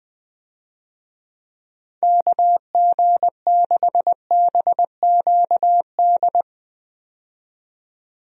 Morse code, heard as a keyed tone, about 20 wpm, 700 Hz. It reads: KG6BQD
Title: USA Call Signs - 6 Characters - 20wpm